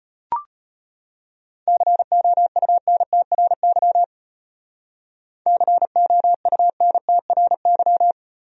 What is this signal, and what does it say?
Morse code, heard as a keyed tone, about 30 wpm, 700 Hz. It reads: COUNTRY COUNTRY
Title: Speed-Racing Top 300 Words 25wpm